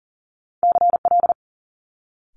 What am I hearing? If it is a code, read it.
Morse code, heard as a keyed tone, about 38 wpm, 700 Hz. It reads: CL
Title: Multi-letter Phonemes Quick Fire 40wpm